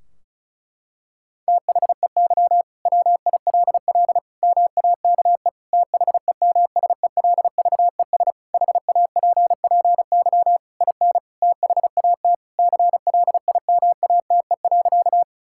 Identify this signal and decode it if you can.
Morse code, heard as a keyed tone, about 36 wpm, 700 Hz. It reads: THEY WILL MAKE THEMSELVES HAPPY IN THAT CLIMATE.
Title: Sentences from Top 1600 Words 35wpm